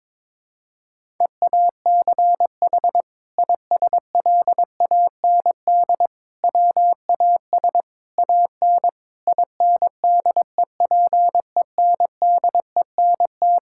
Morse code, heard as a keyed tone, about 22 wpm, 700 Hz. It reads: EACH ISLAND WAS AN INDEPENDENT
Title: Sentences from Top 1000 Words - Review Part 6 - 22wpm